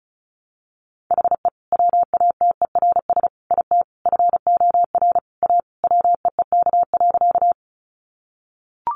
Morse code, heard as a keyed tone, about 34 wpm, 700 Hz. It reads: HE WATERS IT FOR A WEEK.
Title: Easy Phrases 35wpm